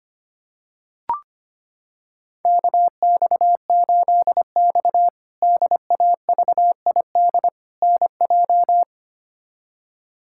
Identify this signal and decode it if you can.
Morse code, heard as a keyed tone, about 25 wpm, 700 Hz. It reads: KX8X DAVID NJ
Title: K1USN SST QSO Practice - 25wpm